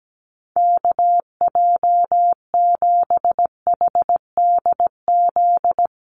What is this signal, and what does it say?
Morse code, heard as a keyed tone, about 17 wpm, 700 Hz. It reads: KJ7HDZ
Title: USA Call Signs Repeated 3x 17wpm